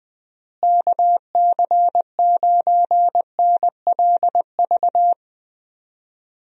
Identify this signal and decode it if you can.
Morse code, heard as a keyed tone, about 20 wpm, 700 Hz. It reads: KC9NLV
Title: USA Call Signs - 6 Characters - Repeated 3x - 20wpm